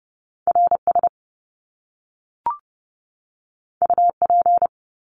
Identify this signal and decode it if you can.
Morse code, heard as a keyed tone, about 30 wpm, 700 Hz. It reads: RS UP
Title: Top Two Letter Combinations in Words 30wpm